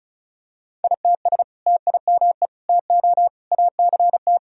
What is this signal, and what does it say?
Morse code, heard as a keyed tone, about 36 wpm, 700 Hz. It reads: ITS TIME TO ACT
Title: Sentences from Top 1000 Words - Review Part 1 - 35wpm